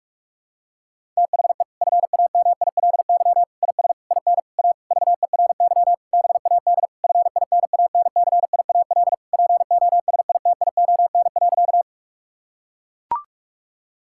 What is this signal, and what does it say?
Morse code, heard as a keyed tone, about 41 wpm, 700 Hz. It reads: THE FAMILY IS IN A VERY BAD FINANCIAL POSITION.
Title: Sentences from Top 1000 Words - Review Part 5 - 45wpm